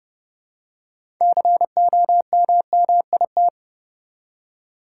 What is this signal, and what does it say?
Morse code, heard as a keyed tone, about 30 wpm, 700 Hz. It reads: COMMIT
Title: Six Letter Words - 30wpm